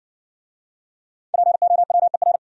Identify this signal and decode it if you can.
Morse code, heard as a keyed tone, about 60 wpm, 700 Hz. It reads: POWER